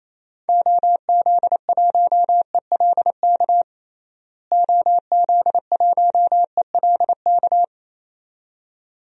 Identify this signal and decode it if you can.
Morse code, heard as a keyed tone, about 28 wpm, 700 Hz. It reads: OZ1ELK OZ1ELK
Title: DXCC-100 Call Signs - 6 Characters - Repeated 2x - 28wpm